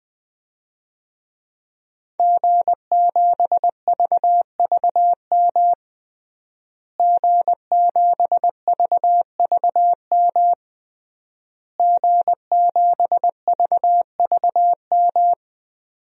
Morse code, heard as a keyed tone, about 20 wpm, 700 Hz. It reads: G7VVM G7VVM G7VVM
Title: DXCC-100 Call Signs - 5 Characters - Repeated 3x - 20wpm